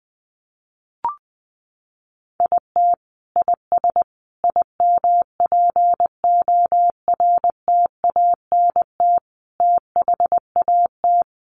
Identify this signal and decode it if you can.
Morse code, heard as a keyed tone, about 20 wpm, 700 Hz. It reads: IT IS IMPORTANT THAT
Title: Sentences from Top 600 Words 20wpm